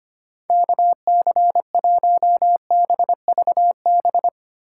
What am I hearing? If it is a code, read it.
Morse code, heard as a keyed tone, about 33 wpm, 700 Hz. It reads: KC1BVB